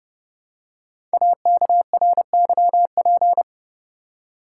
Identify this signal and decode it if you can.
Morse code, heard as a keyed tone, about 28 wpm, 700 Hz. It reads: AKRYP
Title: Code Groups - 5 Letters - 30wpm